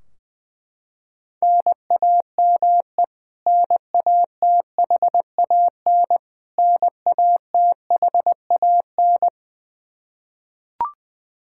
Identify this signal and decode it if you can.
Morse code, heard as a keyed tone, about 20 wpm, 700 Hz. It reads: NAME NATHAN NATHAN
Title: Practice QSO - Exchange 1 - Beginning - 20wpm